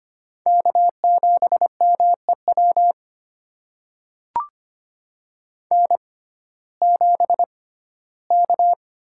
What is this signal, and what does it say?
Morse code, heard as a keyed tone, about 25 wpm, spaced as for 15 wpm, 700 Hz. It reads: K7MEW N7K
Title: USA Call Signs - 6x Character Spacing - 25wpm